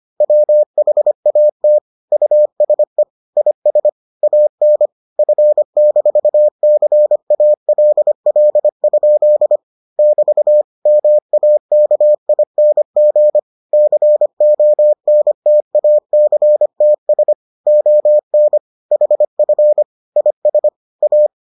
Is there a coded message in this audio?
Yes: Morse code from 0.2 to 21.3 s, heard as a keyed tone, about 25 wpm, 600 Hz. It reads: WHAT USE IS AN F-CALL? <BT> MAKING CONTACTS ON HF IS A